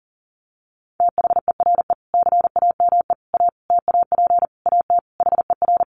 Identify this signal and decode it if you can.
Morse code, heard as a keyed tone, about 40 wpm, 700 Hz. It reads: THERE CAME A TAP AT HER
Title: Sentences from Top 1400 Words 40wpm